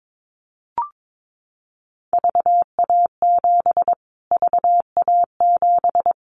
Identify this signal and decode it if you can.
Morse code, heard as a keyed tone, about 22 wpm, 700 Hz. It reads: VA7 VA7
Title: Callsign Prefixes for Top 25 DXCC - Repeated 3x - 22wpm